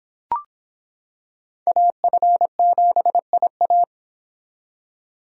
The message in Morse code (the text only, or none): AF7IA